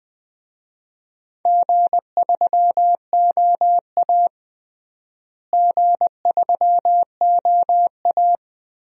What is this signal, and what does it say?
Morse code, heard as a keyed tone, about 20 wpm, 700 Hz. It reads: G3OA G3OA
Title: DXCC-100 Call Signs - 4 Characters - Repeated 3x - 20wpm